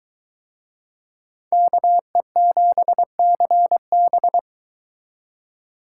Morse code, heard as a keyed tone, about 20 wpm, 700 Hz. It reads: KE7CB